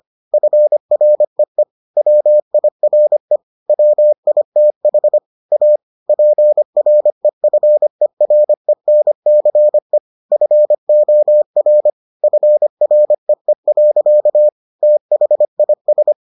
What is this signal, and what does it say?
Morse code, heard as a keyed tone, about 25 wpm, 600 Hz. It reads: FREE WIRE WITH A PREFERENCE FOR FREE. THIS